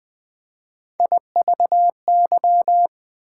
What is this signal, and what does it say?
Morse code, heard as a keyed tone, about 20 wpm, 700 Hz. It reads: IVY
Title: Difficult Words with Hints - 20wpm